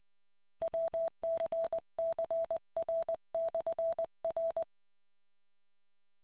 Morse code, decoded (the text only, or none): WCCR/R